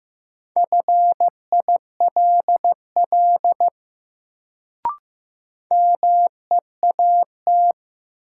Morse code, heard as a keyed tone, about 15 wpm, 700 Hz. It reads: FILL MEAT
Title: Sentences from Top 1000 Words - Review Part 3 - New Word Drill - 15wpm